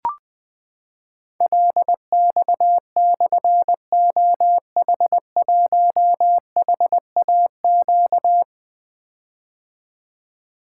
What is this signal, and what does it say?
Morse code, heard as a keyed tone, about 20 wpm, 700 Hz. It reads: LX/OH1HAQ
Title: Difficult Call Signs 20wpm